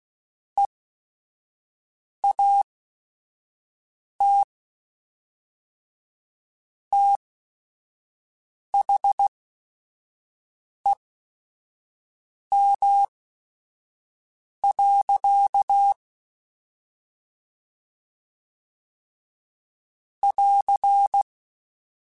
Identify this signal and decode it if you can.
Morse code, heard as a keyed tone, about 16 wpm, spaced as for 5 wpm, 800 Hz. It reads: EAT THEM. <AR>
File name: Chuckles / Chuckles-mp3s / SoftWords @5